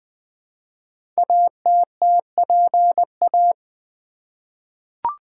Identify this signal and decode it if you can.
Morse code, heard as a keyed tone, about 20 wpm, 700 Hz. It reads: ATTPA
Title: Code Groups - 5 Letters - 20wpm